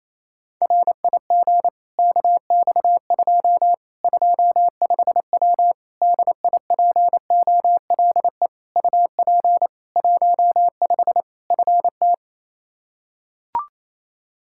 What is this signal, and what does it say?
Morse code, heard as a keyed tone, about 28 wpm, 700 Hz. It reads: RIG KX2 25W DIPOLE UP 15 FT
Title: Practice QSO - Exchange 2 - Info - 28wpm